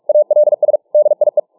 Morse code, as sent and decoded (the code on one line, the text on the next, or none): .- .-.. ... / -.. .. .
ALS DIE